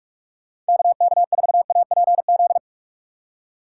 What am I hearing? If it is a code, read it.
Morse code, heard as a keyed tone, about 57 wpm, 700 Hz. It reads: KK4APZ